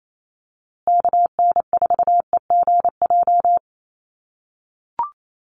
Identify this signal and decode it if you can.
Morse code, heard as a keyed tone, about 28 wpm, 700 Hz. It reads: KN4EGJ
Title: USA Call Signs Repeated 3x 28wpm